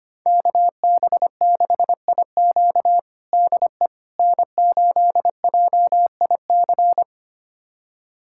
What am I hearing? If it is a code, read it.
Morse code, heard as a keyed tone, about 25 wpm, 700 Hz. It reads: KB6IQ DE N8JIC
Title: USA Callsign DE Callsign - 5 Characters - 25wpm